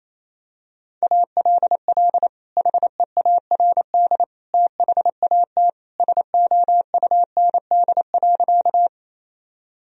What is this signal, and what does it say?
Morse code, heard as a keyed tone, about 28 wpm, 700 Hz. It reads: ALL HEARD THAT SOUND.
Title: Sentences from Top 1000 Words - Review Part 1 - 28wpm